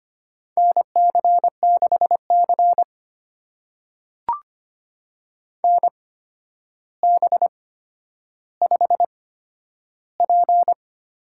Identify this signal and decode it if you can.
Morse code, heard as a keyed tone, about 25 wpm, spaced as for 11 wpm, 700 Hz. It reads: NC6C NB5P
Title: USA Call Signs - 4 Characters - 8x Character Spacing - 25wpm